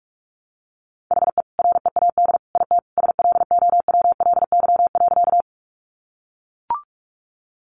Morse code, heard as a keyed tone, about 44 wpm, 700 Hz. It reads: HE READ IT SLOWLY.
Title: Sentences from Top 1100 Words 45wpm